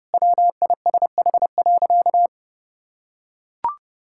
Morse code, heard as a keyed tone, about 32 wpm, 700 Hz. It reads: WISH.
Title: Sentences from Top 1600 Words 30wpm